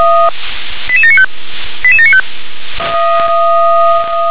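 Five-tone selective call (ZVEI), heard as quick runs of five tones, tones of about 70 ms. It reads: 88795 88795
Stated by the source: ZVEI five-tone selective call 88692, received on an SDR.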